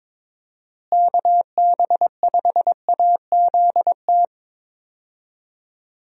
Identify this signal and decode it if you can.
Morse code, heard as a keyed tone, about 22 wpm, 700 Hz. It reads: KB5AZT
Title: USA Call Signs - 6 Characters - Repeated 3x - 22wpm